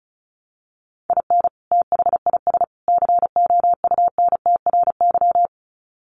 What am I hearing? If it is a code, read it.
Morse code, heard as a keyed tone, about 34 wpm, 700 Hz. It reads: IN THIS COUNTRY
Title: Sets of 3 Words 35wpm